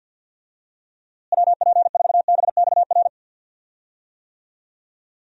Morse code, heard as a keyed tone, about 47 wpm, 700 Hz. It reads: WP4BXR
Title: USA Call Signs - 6 Characters - Repeated 2x - 50wpm